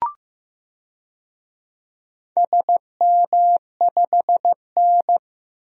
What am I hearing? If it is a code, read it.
Morse code, heard as a keyed tone, about 15 wpm, 700 Hz. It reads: SM5N
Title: DXCC-100 Call Signs - 4 Characters - 15wpm